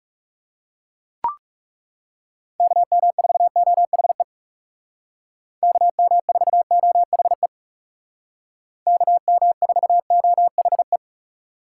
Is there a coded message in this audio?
Yes: Morse code from 2.6 to 11.0 s, heard as a keyed tone, about 39 wpm, 700 Hz. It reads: KM4OHE KM4OHE KM4OHE